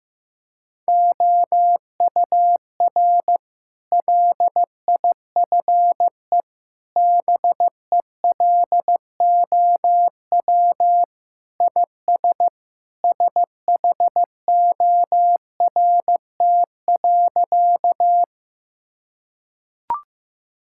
Morse code, heard as a keyed tone, about 15 wpm, 700 Hz. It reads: OUR LIFE BELOW IS SHORT.
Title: Sentences from Top 1000 Words - Review Part 5 - 15wpm